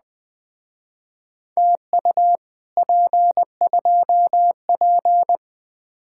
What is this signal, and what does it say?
Morse code, heard as a keyed tone, about 20 wpm, 700 Hz. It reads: TU P2P
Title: Speed-Racing Common QSO Phrases - 17wpm